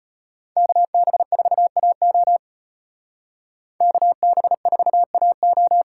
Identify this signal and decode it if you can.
Morse code, heard as a keyed tone, about 32 wpm, 700 Hz. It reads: KB4AO KB4AO